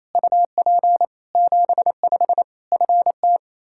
Morse code, heard as a keyed tone, about 28 wpm, 700 Hz. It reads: UP 75 FT
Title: Practice QSO - Exchange 2 - Info - 28wpm